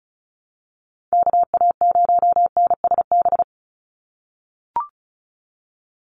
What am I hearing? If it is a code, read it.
Morse code, heard as a keyed tone, about 34 wpm, 700 Hz. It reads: KA0NSB
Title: USA Call Signs 35wpm